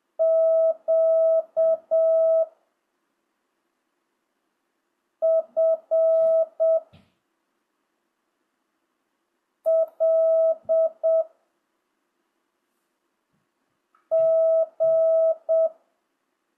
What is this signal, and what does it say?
Morse code, heard as a keyed tone, about 7 wpm, 1300 Hz. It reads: Q F L G
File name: jh8cbh / 2025cw